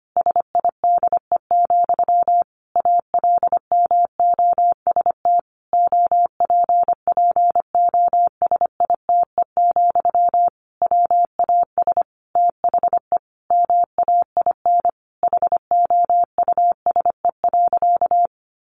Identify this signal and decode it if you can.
Morse code, heard as a keyed tone, about 25 wpm, 700 Hz. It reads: SIDE, ALMOST OPPOSITE, WAS THE MAIN HOUSE.
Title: Sentences from Top 1300 Words 25wpm